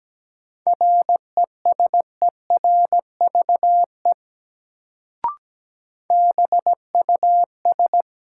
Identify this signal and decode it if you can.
Morse code, heard as a keyed tone, about 17 wpm, 700 Hz. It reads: RESERVE BUS
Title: Sentences from Top 1000 Words - Review Part 6 - New Word Drill - 17wpm